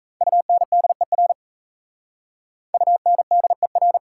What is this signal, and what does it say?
Morse code, heard as a keyed tone, about 37 wpm, 700 Hz. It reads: UNDER UNDER